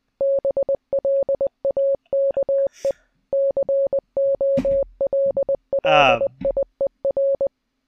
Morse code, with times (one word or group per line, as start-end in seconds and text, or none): 0.2-2.9 BLAKE
3.3-7.5 COLLIER